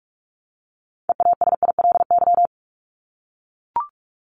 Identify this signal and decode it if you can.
Morse code, heard as a keyed tone, about 44 wpm, 700 Hz. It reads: EASILY
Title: Top 751-1000 Words 45wpm